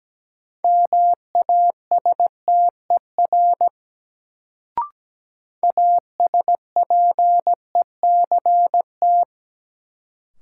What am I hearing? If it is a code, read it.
Morse code, heard as a keyed tone, about 17 wpm, 700 Hz. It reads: MASTER ASPECT
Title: Sentences from Top 1300 Words - New Word Drill Alt - 17wpm